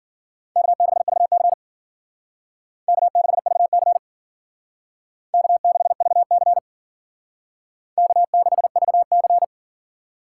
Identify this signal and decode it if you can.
Morse code, heard as a keyed tone, about 52 wpm, 700 Hz. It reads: K6VC K6VC K6VC K6VC